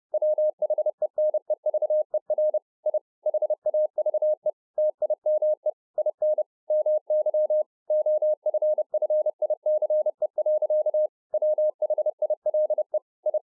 Morse code, heard as a keyed tone, about 30 wpm, 610 Hz. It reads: WHENEVER I HAVE TIME IN MY OFFICE. WHILE I